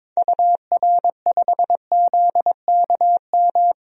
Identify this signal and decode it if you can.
Morse code, heard as a keyed tone, about 22 wpm, 700 Hz. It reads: UR5ZKM
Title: DXCC-100 Call Signs - 6 Characters - Repeated 3x - 22wpm